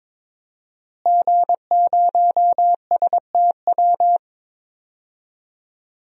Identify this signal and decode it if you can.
Morse code, heard as a keyed tone, about 22 wpm, 700 Hz. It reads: G0STW
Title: DXCC-100 Call Signs - 5 Characters - Repeated 3x - 22wpm